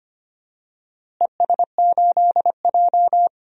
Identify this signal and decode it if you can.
Morse code, heard as a keyed tone, about 25 wpm, 700 Hz. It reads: ES8J
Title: DXCC-100 Call Signs - 4 Characters - Repeated 2x - 25wpm